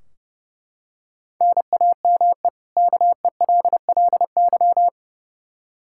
Morse code, heard as a keyed tone, about 30 wpm, 700 Hz. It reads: NAME KELLY
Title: Practice QSO - Exchange 1 - Beginning - 30wpm